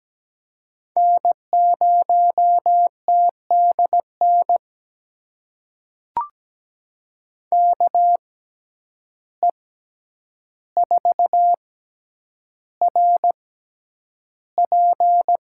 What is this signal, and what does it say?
Morse code, heard as a keyed tone, about 17 wpm, spaced as for 9 wpm, 700 Hz. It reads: N0TDN KE4RP
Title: USA Call Signs - 6x Character Spacing - 17wpm